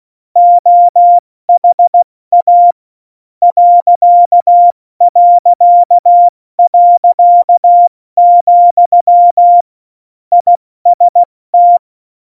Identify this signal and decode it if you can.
Morse code, heard as a keyed tone, about 16 wpm, 700 Hz. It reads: OHA ..., IST
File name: Morsekurs / Witze / Witz 05 Mai, 2020 70 F80